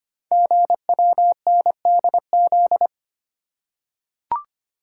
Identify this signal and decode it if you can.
Morse code, heard as a keyed tone, about 25 wpm, 700 Hz. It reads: GWNDZ
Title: Code Groups - 5 Letters - 25wpm